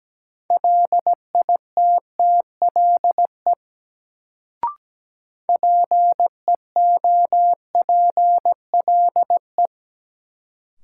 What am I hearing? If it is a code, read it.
Morse code, heard as a keyed tone, about 17 wpm, 700 Hz. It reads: LITTLE PEOPLE
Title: Sets of 2 Words - New Word Drill - 17wpm